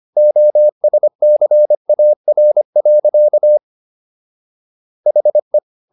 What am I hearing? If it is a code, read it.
Morse code, heard as a keyed tone, about 25 wpm, 600 Hz. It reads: OSCAR. HE